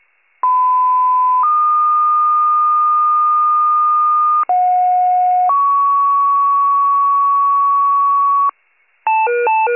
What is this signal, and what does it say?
Two-tone page, heard as a one-second tone, then a three-second tone, about 1010 then 1220 Hz, about 730 then 1120 Hz. It is heard from 0.4 to 8.5 s.
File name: pager / 56 RESCUE / wcalert